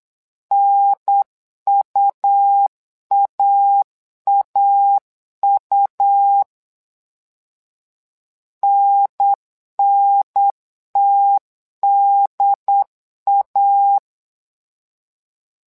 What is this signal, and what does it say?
Morse code, heard as a keyed tone, about 8 wpm, 800 Hz. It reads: NUAAU NNTDA
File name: LEZIONI / 2 UHDN / 2 UHDN 8wpm2232